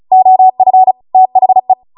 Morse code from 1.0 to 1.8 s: THE